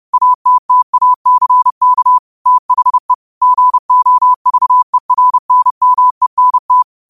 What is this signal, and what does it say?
Morse code, heard as a keyed tone, about 30 wpm, 1000 Hz. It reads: ATTACK THE GOVERNMENT
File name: qotdfiles / 30wpm